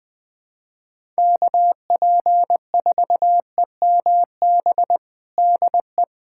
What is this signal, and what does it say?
Morse code, heard as a keyed tone, about 20 wpm, 700 Hz. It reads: KP4EMB DE
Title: USA Callsign DE Callsign 20wpm